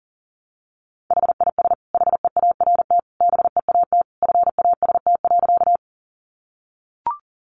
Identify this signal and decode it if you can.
Morse code, heard as a keyed tone, about 40 wpm, 700 Hz. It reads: HIS HEART BEAT FAST.
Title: Sentences from Top 1300 Words 40wpm